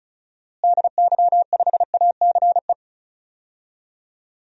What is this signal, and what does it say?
Morse code, heard as a keyed tone, about 36 wpm, 700 Hz. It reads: DY5ACE
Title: DXCC-100 Call Signs - 6 Characters - Repeated 2x - 35wpm